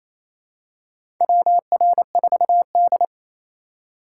Morse code, heard as a keyed tone, about 28 wpm, 700 Hz. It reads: WR4D
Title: USA Call Signs - 6x Character Spacing - 28wpm